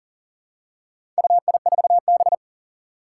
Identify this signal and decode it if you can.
Morse code, heard as a keyed tone, about 38 wpm, 700 Hz. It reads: UI4B
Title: Four Challenging Letter-Number Combinations 40wpm